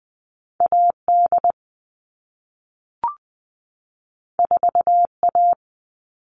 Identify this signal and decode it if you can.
Morse code, heard as a keyed tone, about 20 wpm, 700 Hz. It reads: AD 4A
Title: All Cut Numbers - 2-Digit 20wpm